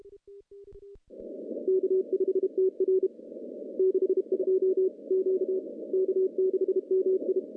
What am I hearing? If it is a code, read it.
Morse code, heard as a keyed tone, about 32 wpm, 410 Hz. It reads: K5TR 62 Q K6MI
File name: contests / CW-SS-k5tr- -0200